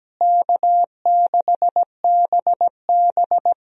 Morse code, heard as a keyed tone, about 17 wpm, 700 Hz. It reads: K6BB
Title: USA Call Signs - 4 Characters - Repeated 3x - 17wpm